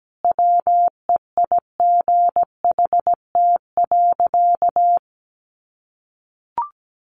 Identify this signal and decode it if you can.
Morse code, heard as a keyed tone, about 17 wpm, 700 Hz. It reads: WEIGHT.
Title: Sentences from Top 800 Words 17wpm